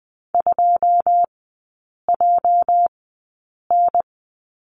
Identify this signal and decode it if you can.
Morse code, heard as a keyed tone, about 20 wpm, spaced as for 8 wpm, 700 Hz. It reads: 2JN
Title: 205 Lesson 33e - Letter j - Three Characters 20wpm